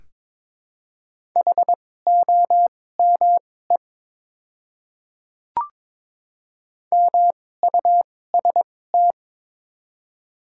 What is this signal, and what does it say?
Morse code, heard as a keyed tone, about 22 wpm, spaced as for 16 wpm, 700 Hz. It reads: HOME MUST
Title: 2x Character Spacing - Top 300 Words - 22wpm